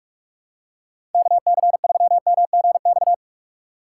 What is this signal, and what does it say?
Morse code, heard as a keyed tone, about 44 wpm, 700 Hz. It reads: KC3MGX
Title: USA Call Signs - 6 Characters - 2x Character Spacing - 45wpm